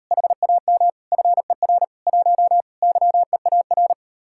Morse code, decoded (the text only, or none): HAM FER 1 YEAR